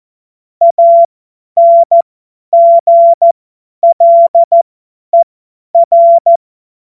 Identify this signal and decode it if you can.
Morse code, heard as a keyed tone, about 14 wpm, spaced as for 10 wpm, 680 Hz. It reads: ANGLER
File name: Morsekurs / Witze / Witz 15. Februar 2022 50 F70